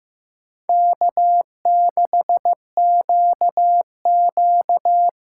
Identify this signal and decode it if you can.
Morse code, heard as a keyed tone, about 18 wpm, 700 Hz. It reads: K6QQ
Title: Speed Racing - USA 1x2 Call Signs 15wpm